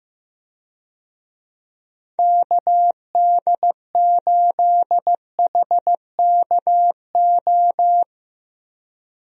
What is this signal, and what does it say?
Morse code, heard as a keyed tone, about 15 wpm, 700 Hz. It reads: KD8HKO
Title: USA Call Signs - 6 Characters - Repeated 3x - 15wpm